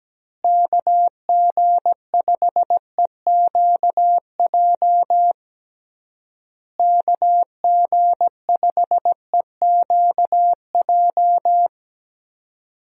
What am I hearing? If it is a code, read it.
Morse code, heard as a keyed tone, about 17 wpm, 700 Hz. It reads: KG5EQJ KG5EQJ
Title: USA Call Signs - 6 Characters - Repeated 3x - 17wpm